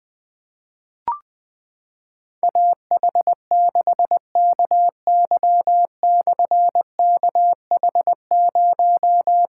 Morse code, heard as a keyed tone, about 20 wpm, 700 Hz. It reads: AH6KY/KH0
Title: Difficult Call Signs 20wpm